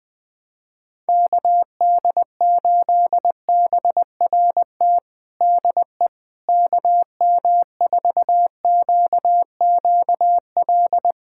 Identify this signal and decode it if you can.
Morse code, heard as a keyed tone, about 20 wpm, 700 Hz. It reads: KD8BRT DE KM4QQL
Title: USA Callsign DE Callsign 20wpm